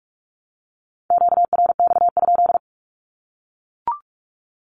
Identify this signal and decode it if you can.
Morse code, heard as a keyed tone, about 44 wpm, 700 Hz. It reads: QRX?
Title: Q-Codes Repeated 3x 45wpm